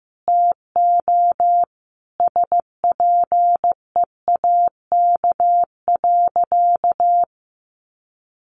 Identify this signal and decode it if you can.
Morse code, heard as a keyed tone, about 15 wpm, 700 Hz. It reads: TO SPEAK.
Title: Sentences from Top 1100 Words 15wpm